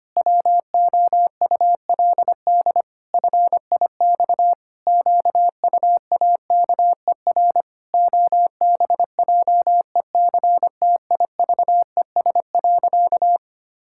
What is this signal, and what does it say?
Morse code, heard as a keyed tone, about 25 wpm, 700 Hz. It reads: WOULD FIX QUAKER OBJECTIVES.
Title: Pangram Challenge v2 - 25wpm